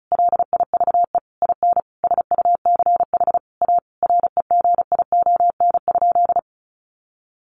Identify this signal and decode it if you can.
Morse code, heard as a keyed tone, about 34 wpm, 700 Hz. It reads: LIVE IN SUCH A REGION?
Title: Sentences from Top 1500 Words 35wpm